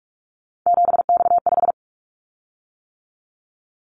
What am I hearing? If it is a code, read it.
Morse code, heard as a keyed tone, about 46 wpm, 700 Hz. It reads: 7X5
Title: Callsign Prefixes for Top 100 DXCC - 45wpm